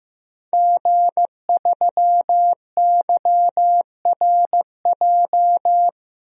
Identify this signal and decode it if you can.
Morse code, heard as a keyed tone, about 15 wpm, 700 Hz. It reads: G3YRJ
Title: DXCC-100 Call Signs - 5 Characters - 15wpm